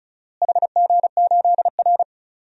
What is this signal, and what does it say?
Morse code, heard as a keyed tone, about 36 wpm, 700 Hz. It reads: HG8R
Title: DXCC-100 Call Signs - 4 Characters - Repeated 3x - 35wpm